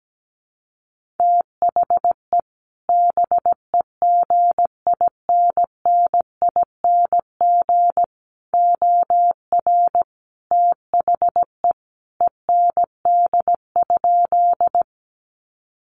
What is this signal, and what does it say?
Morse code, heard as a keyed tone, about 17 wpm, 700 Hz. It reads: THE BEGINNING OR THE END?
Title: Sentences from Top 800 Words 17wpm